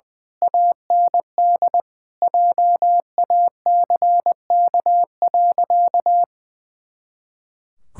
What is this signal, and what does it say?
Morse code, heard as a keyed tone, about 21 wpm, 700 Hz. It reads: AND JACK.
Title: Pangram Challenge v2 - 20wpm